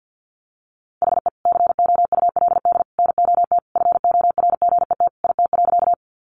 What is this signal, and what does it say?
Morse code, heard as a keyed tone, about 49 wpm, 700 Hz. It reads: HE COULD NOT FORGET IT.
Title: Sentences from Top 1200 Words 50wpm